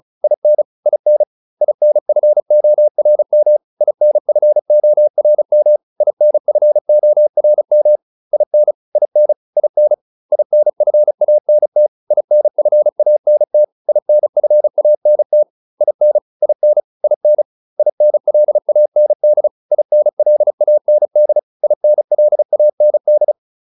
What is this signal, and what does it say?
Morse code, heard as a keyed tone, about 34 wpm, 600 Hz. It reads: IN IN INFORM INFORM INFORM IN IN IN INFANT INFANT INFANT IN IN IN INLAND INLAND INLAND
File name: IN6 35